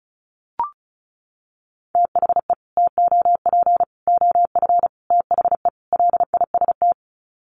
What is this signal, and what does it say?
Morse code, heard as a keyed tone, about 34 wpm, 700 Hz. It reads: THE TOP OF THE LIST
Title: Sets of 5 Words Encore2 35wpm